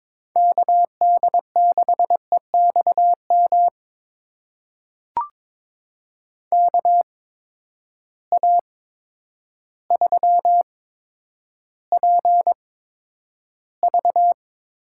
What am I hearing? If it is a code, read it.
Morse code, heard as a keyed tone, about 22 wpm, spaced as for 10 wpm, 700 Hz. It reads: KD6EXM KA3PV
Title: USA Call Signs - 6 Characters - 8x Character Spacing - 22wpm